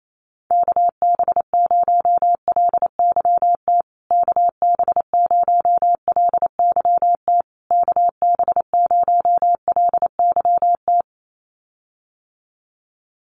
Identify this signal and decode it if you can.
Morse code, heard as a keyed tone, about 28 wpm, 700 Hz. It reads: KB0LYT KB0LYT KB0LYT
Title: USA Call Signs Repeated 3x 28wpm